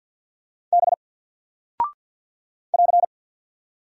Morse code, heard as a keyed tone, about 48 wpm, 700 Hz. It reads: B <AR>
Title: Single Character (Letter:Number:Punctuation:Prosign) 50wpm